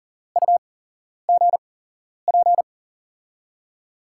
Morse code, heard as a keyed tone, about 40 wpm, spaced as for 11 wpm, 700 Hz. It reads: UGP
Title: USA Call Signs - 8x Character Spacing - 40wpm